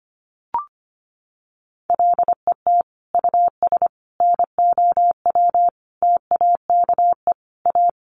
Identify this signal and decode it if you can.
Morse code, heard as a keyed tone, about 25 wpm, 700 Hz. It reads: LET US NOW TAKE A
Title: Sentences from Top 900 Words 25wpm